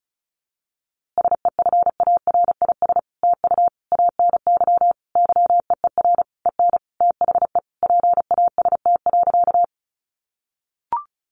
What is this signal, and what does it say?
Morse code, heard as a keyed tone, about 34 wpm, 700 Hz. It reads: SEFARIS TU ANY YEER EN THE PAST.
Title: A Sound Of Thunder 35wpm